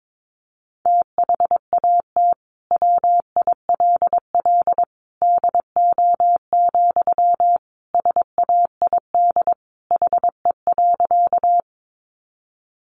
Morse code, heard as a keyed tone, about 22 wpm, 700 Hz. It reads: THAT WILL DO, SAID HE.
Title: Sentences from Top 100 Words 22wpm